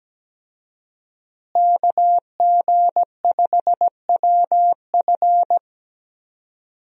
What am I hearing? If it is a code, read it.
Morse code, heard as a keyed tone, about 17 wpm, 700 Hz. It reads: KG5WF
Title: USA Call Signs - 5 Characters - Repeated 3x - 17wpm